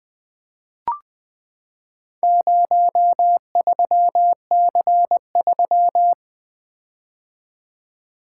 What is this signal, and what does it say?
Morse code, heard as a keyed tone, about 20 wpm, 700 Hz. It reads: 03C3
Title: Four Letter-Number Combinations - 20wpm